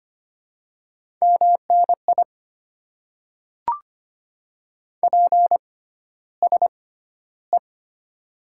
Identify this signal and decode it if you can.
Morse code, heard as a keyed tone, about 25 wpm, spaced as for 13 wpm, 700 Hz. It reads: MNI PSE
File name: Top 90 QSO Elements - 6x Character Spacing - 25wpm